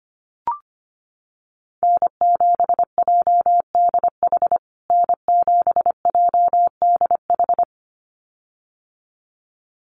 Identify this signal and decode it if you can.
Morse code, heard as a keyed tone, about 25 wpm, 700 Hz. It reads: N7JDH N7JDH
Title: USA Call Signs Repeated 2x 25wpm